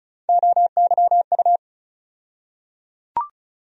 Morse code, heard as a keyed tone, about 34 wpm, 700 Hz. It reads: OYU
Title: USA Call Sign Postfixes - 3 Letters 35wpm